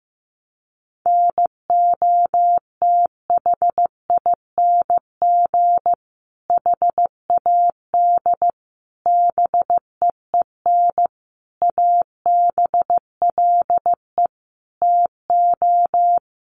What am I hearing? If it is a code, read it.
Morse code, heard as a keyed tone, about 15 wpm, 700 Hz. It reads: NOTHING HAD BEEN ABLE TO
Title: Sentences from Top 1300 Words 15wpm